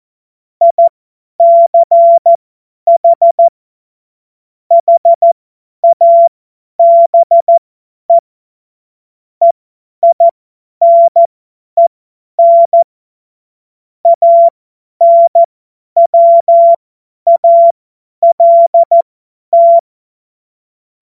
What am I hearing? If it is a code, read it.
Morse code, heard as a keyed tone, about 14 wpm, spaced as for 10 wpm, 680 Hz. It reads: ICH HABE EINEN ANWALT
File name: Morsekurs / Witze / Witz 22. August 2023 50 F70